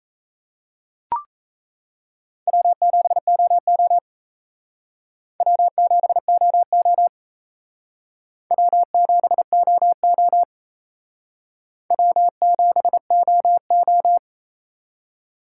Morse code, heard as a keyed tone, about 35 wpm, 700 Hz. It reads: W7OO W7OO W7OO W7OO
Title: Speed Racing - USA 1x2 Call Signs 28wpm